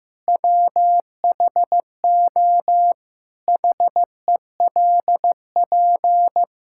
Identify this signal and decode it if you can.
Morse code, heard as a keyed tone, about 15 wpm, 700 Hz. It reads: WHO HELP
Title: Sentences from Top 1000 Words - Review Part 5 - 15wpm